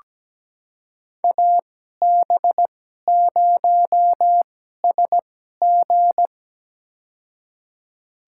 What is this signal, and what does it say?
Morse code, heard as a keyed tone, about 17 wpm, spaced as for 12 wpm, 700 Hz. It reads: AB0SG
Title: USA Call Signs - 2x Character Spacing - 17wpm